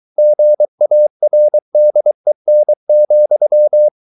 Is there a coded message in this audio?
Yes: Morse code from 0.2 to 3.9 s, heard as a keyed tone, about 23 wpm, 600 Hz. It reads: GARDEN,